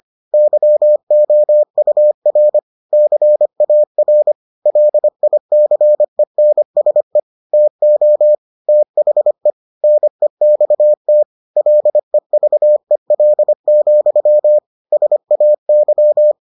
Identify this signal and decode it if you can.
Morse code, heard as a keyed tone, about 25 wpm, 600 Hz. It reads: YOUR CAR LICENSE TO THE NEXT LEVEL, SAY